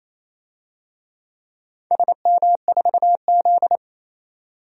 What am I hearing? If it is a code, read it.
Morse code, heard as a keyed tone, about 28 wpm, 700 Hz. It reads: SM4Z
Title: DXCC-100 Call Signs - 4 Characters - 28wpm